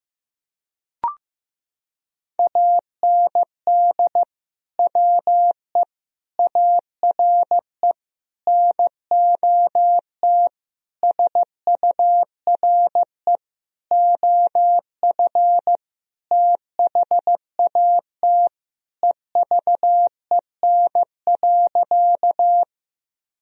Morse code, heard as a keyed tone, about 15 wpm, 700 Hz. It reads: AND WE ARE NOT SURE OF THAT EVEN.
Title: Sentences from Top 1000 Words - Review Part 1 - 15wpm